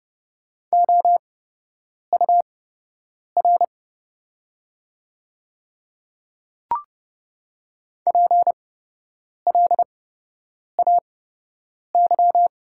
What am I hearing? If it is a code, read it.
Morse code, heard as a keyed tone, about 30 wpm, spaced as for 8 wpm, 700 Hz. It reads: OUR PLAY
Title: 8x Character Spacing - Top 200 Words - 30wpm